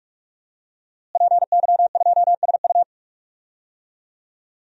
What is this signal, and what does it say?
Morse code, heard as a keyed tone, about 47 wpm, 700 Hz. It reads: PY2SU
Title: DXCC-100 Call Signs - 5 Characters - Repeated 2x - 45wpm